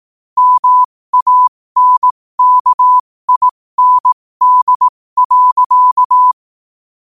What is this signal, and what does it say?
Morse code, heard as a keyed tone, about 18 wpm, spaced as for 15 wpm, 1000 Hz. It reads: MANKIND.
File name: qotdfiles / 15wpm